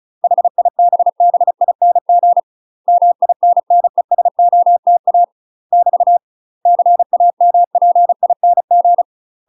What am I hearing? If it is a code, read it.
Morse code, heard as a keyed tone, about 34 wpm, 700 Hz. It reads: HIBBING MINNESOTA <BT> CAMPING
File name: UK MASTER QSOs TRAINING / 35WPM / UK MASTER QSOs 0034